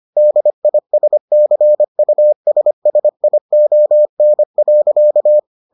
Morse code, heard as a keyed tone, about 25 wpm, 600 Hz. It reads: DISCUSSION.